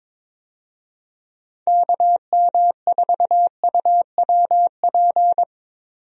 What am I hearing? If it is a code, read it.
Morse code, heard as a keyed tone, about 21 wpm, 700 Hz. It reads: KM4UWP
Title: USA Call Signs - 6 Characters - Repeated 3x - 22wpm